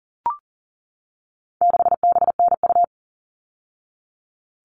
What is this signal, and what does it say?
Morse code, heard as a keyed tone, about 40 wpm, 700 Hz. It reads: 6BNU
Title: All Cut Numbers - 4-Digit 40wpm